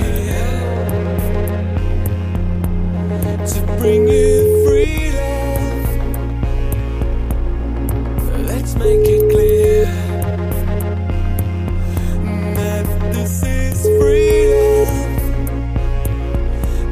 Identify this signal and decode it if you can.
Ringback tone (Europe, 425 Hz), with noise.